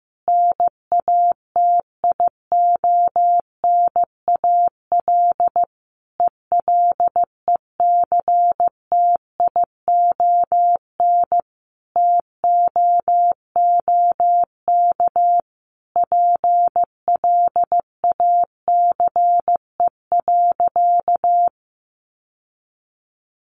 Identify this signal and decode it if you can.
Morse code, heard as a keyed tone, about 15 wpm, 700 Hz. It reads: NATIONAL ELECTION TOOK PLACE.
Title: Sentences from Top 1500 Words 15wpm